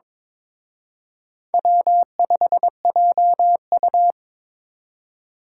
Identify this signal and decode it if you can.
Morse code, heard as a keyed tone, about 26 wpm, 700 Hz. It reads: W5JU
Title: Speed Racing - USA 1x2 Call Signs 22wpm